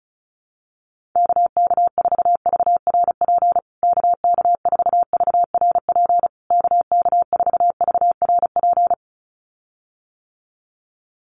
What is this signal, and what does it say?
Morse code, heard as a keyed tone, about 34 wpm, 700 Hz. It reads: KK4VRP KK4VRP KK4VRP
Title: USA Call Signs Repeated 3x 35wpm